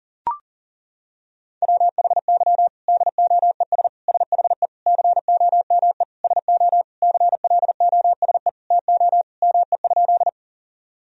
Morse code, heard as a keyed tone, about 37 wpm, 700 Hz. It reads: WHY DOES SHE COME SO CLOSE TO ME?
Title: Sentences from Top 1000 Words - Review Part 1 - 40wpm